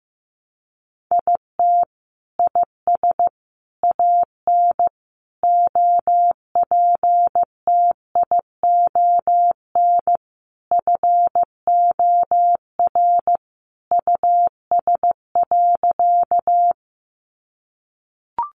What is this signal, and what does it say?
Morse code, heard as a keyed tone, about 15 wpm, 700 Hz. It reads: IT IS AN OPTION FOR US.
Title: Sentences from Top 1300 Words 15wpm